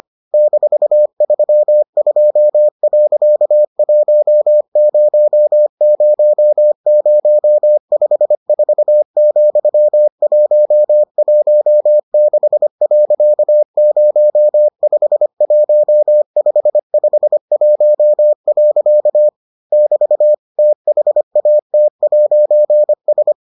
Morse code, heard as a keyed tone, about 25 wpm, 600 Hz. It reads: -32.100054,116.051551. <BT> THAT'S